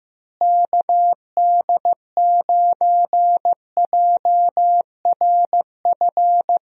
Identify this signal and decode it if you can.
Morse code, heard as a keyed tone, about 15 wpm, 700 Hz. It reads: KD9JRF
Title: USA Call Signs - 6 Characters - Repeated 3x - 15wpm